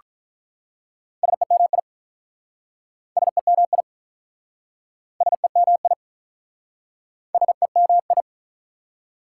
Morse code, heard as a keyed tone, about 47 wpm, 700 Hz. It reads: SEMI SEMI SEMI SEMI